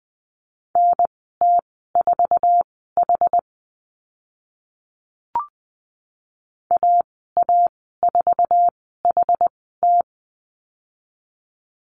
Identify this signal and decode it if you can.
Morse code, heard as a keyed tone, about 20 wpm, spaced as for 14 wpm, 700 Hz. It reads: NT4H AA4HT
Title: 062 Lesson 11g - Letter h - Call Signs 20wpm